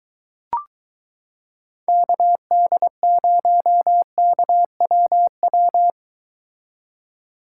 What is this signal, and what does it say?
Morse code, heard as a keyed tone, about 22 wpm, 700 Hz. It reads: KD0KWW